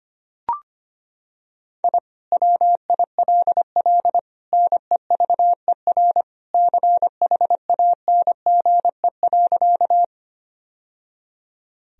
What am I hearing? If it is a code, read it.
Morse code, heard as a keyed tone, about 25 wpm, 700 Hz. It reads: I WILL NEVER CHANGE.
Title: Sentences from Top 1000 Words - Review Part 1 - 25wpm